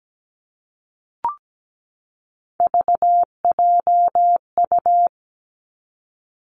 Morse code, heard as a keyed tone, about 17 wpm, 700 Hz. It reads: VJU
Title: Three Challenging Letter-Number Combinations 17wpm